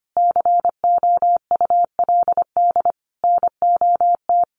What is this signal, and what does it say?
Morse code, heard as a keyed tone, about 25 wpm, 700 Hz. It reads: COULD NOT
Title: Sentences from Top 1400 Words 25wpm